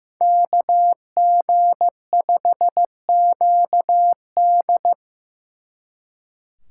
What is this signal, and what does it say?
Morse code, heard as a keyed tone, about 18 wpm, 700 Hz. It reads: KG5QD
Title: Speed Racing - USA 2x2 Call Signs 15wpm